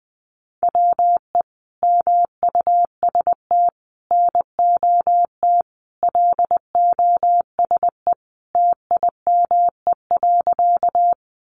Morse code, heard as a keyed tone, about 20 wpm, 700 Hz. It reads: WE MUST NOT LOSE TIME.
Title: Sentences from Top 1100 Words 20wpm